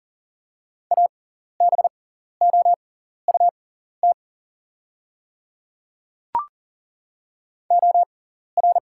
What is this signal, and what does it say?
Morse code, heard as a keyed tone, about 40 wpm, spaced as for 14 wpm, 700 Hz. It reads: ABOUT OR
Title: 6x Character Spacing - Top 100 Words - 40wpm